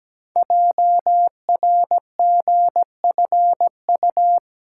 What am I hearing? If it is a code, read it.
Morse code, heard as a keyed tone, about 17 wpm, 700 Hz. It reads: JRGFU
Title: Code Groups - 5 Letters - 17wpm